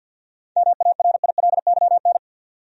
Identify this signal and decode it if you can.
Morse code, heard as a keyed tone, about 51 wpm, 700 Hz. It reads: MARILYN
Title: USA Common Names - 50wpm